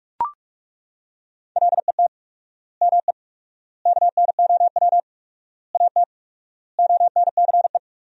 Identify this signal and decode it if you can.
Morse code, heard as a keyed tone, about 41 wpm, 700 Hz. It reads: LET ME KNOW AT ONCE
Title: 4x Word-Spacing - Sentences from Top 300 Words - 45wpm